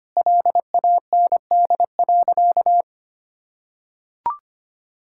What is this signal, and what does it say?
Morse code, heard as a keyed tone, about 25 wpm, 700 Hz. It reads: LAND.
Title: Sentences from Top 1000 Words - Review Part 6 - 25wpm